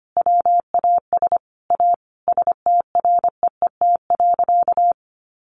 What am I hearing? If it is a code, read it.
Morse code, heard as a keyed tone, about 25 wpm, 700 Hz. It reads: WAS A STREET.
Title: Sentences from Top 1100 Words 25wpm